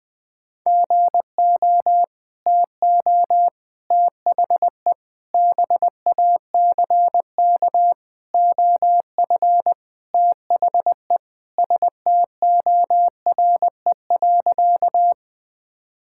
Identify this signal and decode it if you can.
Morse code, heard as a keyed tone, about 21 wpm, 700 Hz. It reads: GO TO THE BACK OF THE STORE.
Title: Sentences from Top 1000 Words - Review Part 5 - 20wpm